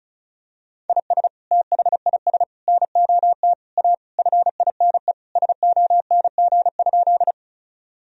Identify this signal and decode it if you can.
Morse code, heard as a keyed tone, about 34 wpm, 700 Hz. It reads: IS THIS NOT A FINE SONG?
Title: Sentences from Top 1000 Words - Review Part 2 - 35wpm